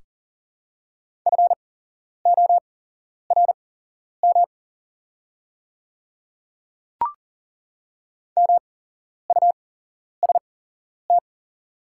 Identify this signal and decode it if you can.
Morse code, heard as a keyed tone, about 42 wpm, spaced as for 11 wpm, 700 Hz. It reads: FORM MUST
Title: 8x Character Spacing - Top 200 Words - 40wpm